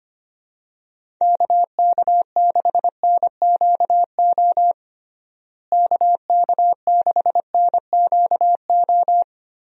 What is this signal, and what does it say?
Morse code, heard as a keyed tone, about 25 wpm, 700 Hz. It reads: KK6NQO KK6NQO
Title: USA Call Signs - 6 Characters - Repeated 2x - 25wpm